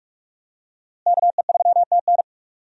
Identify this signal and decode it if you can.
Morse code, heard as a keyed tone, about 46 wpm, 700 Hz. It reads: KE3TN
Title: USA Call Signs - 5 Characters - Repeated 3x - 45wpm